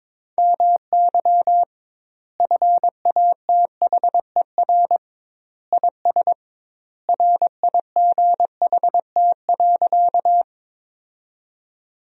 Morse code, heard as a keyed tone, about 22 wpm, 700 Hz. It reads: MY FATHER IS RIGHT.
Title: 2x Word-Spacing - Sentences from Top 300 Words - 22wpm